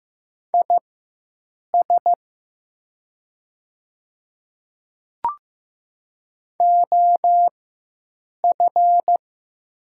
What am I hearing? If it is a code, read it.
Morse code, heard as a keyed tone, about 15 wpm, spaced as for 7 wpm, 700 Hz. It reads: IS OF
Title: 4x Character Spacing - Top 100 Words - 15wpm